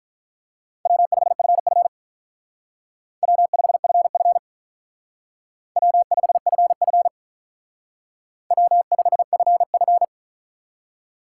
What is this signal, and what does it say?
Morse code, heard as a keyed tone, about 42 wpm, 700 Hz. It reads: W5FF W5FF W5FF W5FF
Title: Speed Racing - USA 1x2 Call Signs 35wpm